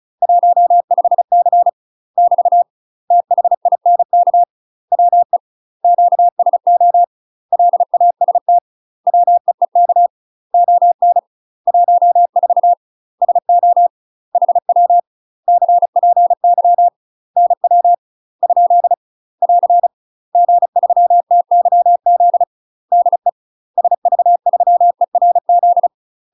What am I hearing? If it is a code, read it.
Morse code, heard as a keyed tone, about 36 wpm, 700 Hz. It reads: 15C <BT> THINK WE QSO LAST WEEK ON 14 SO HW CPY NW ? <AR> G3TYZ DE SV3ERZ
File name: UK MASTER QSOs TRAINING / 35WPM / UK MASTER QSOs 0041